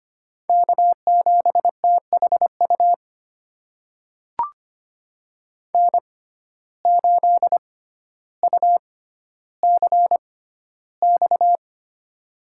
Morse code, heard as a keyed tone, about 25 wpm, spaced as for 13 wpm, 700 Hz. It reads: K7THU N8UCX
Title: USA Call Signs - 5 Characters - 6x Character Spacing - 25wpm